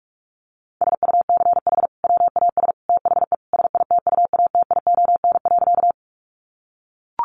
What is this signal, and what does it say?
Morse code, heard as a keyed tone, about 44 wpm, 700 Hz. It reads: SUCH WAS THE SITUATION.
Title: Sentences from Top 1200 Words 45wpm